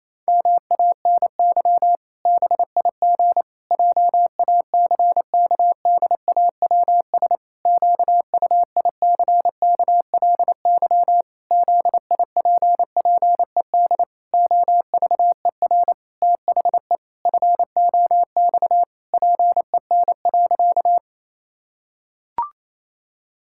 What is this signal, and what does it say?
Morse code, heard as a keyed tone, about 28 wpm, 700 Hz. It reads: MANY BIG JACKDAWS QUICKLY ZIPPED OVER THE FOX PEN.
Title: Pangram Challenge v2 - 28wpm